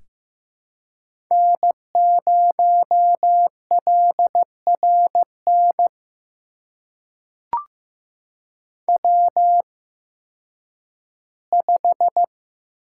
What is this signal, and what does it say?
Morse code, heard as a keyed tone, about 15 wpm, spaced as for 10 wpm, 700 Hz. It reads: N0LRN W5
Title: USA Call Signs - 8x Character Spacing - 15wpm